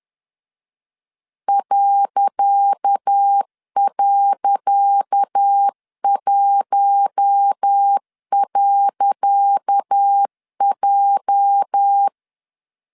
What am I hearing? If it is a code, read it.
Morse code, heard as a keyed tone, about 10 wpm, 800 Hz. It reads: ..1.J
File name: LEZIONI / 10 SIGNS / 10C DOT 10wpm 2940